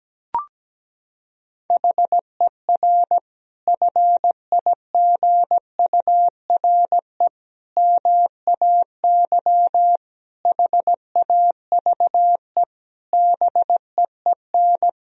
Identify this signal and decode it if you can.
Morse code, heard as a keyed tone, about 17 wpm, 700 Hz. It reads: HER FIGURE MAY HAVE BEEN
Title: Sentences from Top 1000 Words - Review Part 2 - 17wpm